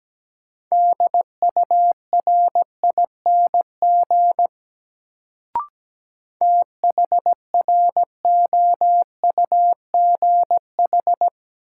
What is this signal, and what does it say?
Morse code, heard as a keyed tone, about 17 wpm, 700 Hz. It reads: DURING THROUGH
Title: Sets of 2 Words - New Word Drill - 17wpm